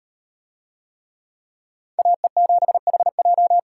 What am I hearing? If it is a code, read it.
Morse code, heard as a keyed tone, about 38 wpm, 700 Hz. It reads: AE7HJ